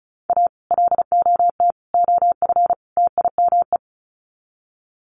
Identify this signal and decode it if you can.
Morse code, heard as a keyed tone, about 34 wpm, 700 Hz. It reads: A LOT OF TIME
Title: Sets of 4 Words 35wpm